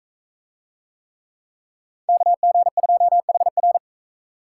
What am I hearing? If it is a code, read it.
Morse code, heard as a keyed tone, about 41 wpm, 700 Hz. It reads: KG2HR